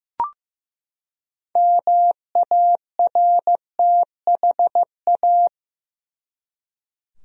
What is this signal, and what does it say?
Morse code, heard as a keyed tone, about 15 wpm, 700 Hz. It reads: MARTHA
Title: USA Common Names - 15wpm